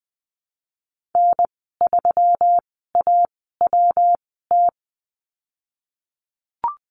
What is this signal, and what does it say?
Morse code, heard as a keyed tone, about 20 wpm, spaced as for 15 wpm, 700 Hz. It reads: N3AWT
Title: 173 Lesson 28g - Number 9 - Call Signs 20wpm